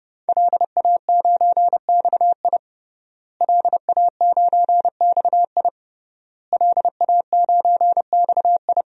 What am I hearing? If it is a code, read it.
Morse code, heard as a keyed tone, about 30 wpm, 700 Hz. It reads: LA9XI LA9XI LA9XI
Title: DXCC-100 Call Signs - 5 Characters - Repeated 3x - 30wpm